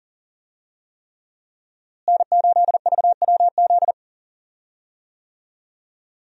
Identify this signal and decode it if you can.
Morse code, heard as a keyed tone, about 37 wpm, 700 Hz. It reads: N8VWZ
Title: USA Call Signs - 5 Characters - Repeated 3x - 40wpm